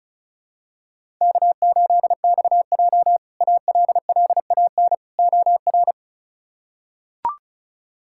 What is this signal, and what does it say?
Morse code, heard as a keyed tone, about 34 wpm, 700 Hz. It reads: K8XJ ALLAN OR
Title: K1USN SST QSO Practice - 35wpm